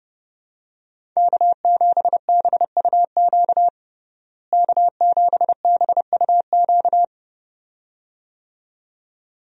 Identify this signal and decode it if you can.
Morse code, heard as a keyed tone, about 32 wpm, 700 Hz. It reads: K7BUQ K7BUQ
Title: USA Call Signs - 5 Characters - Repeated 2x - 30wpm